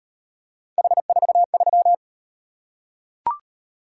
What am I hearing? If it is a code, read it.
Morse code, heard as a keyed tone, about 32 wpm, 700 Hz. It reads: H43